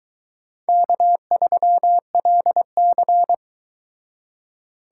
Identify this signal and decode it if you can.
Morse code, heard as a keyed tone, about 22 wpm, 700 Hz. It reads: K3LC